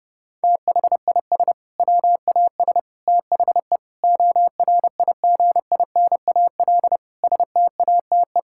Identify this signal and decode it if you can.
Morse code, heard as a keyed tone, about 30 wpm, 700 Hz. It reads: THIS WAS THE ORIGINAL STATE
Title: Sentences from Top 1000 Words - Review Part 6 - 30wpm